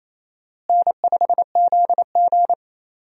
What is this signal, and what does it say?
Morse code, heard as a keyed tone, about 27 wpm, 700 Hz. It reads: N5ZG
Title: USA Call Signs - 4 Characters - Repeated 2x - 28wpm